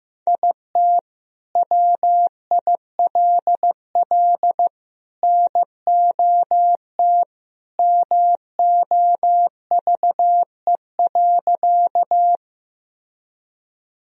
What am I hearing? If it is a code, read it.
Morse code, heard as a keyed tone, about 15 wpm, 700 Hz. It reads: IT WILL NOT MOVE.
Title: Sentences from Top 1000 Words - Review Part 1 - 15wpm